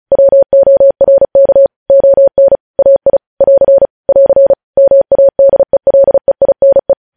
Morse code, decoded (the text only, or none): WORK ON AI <AR> <AR> MADELEINE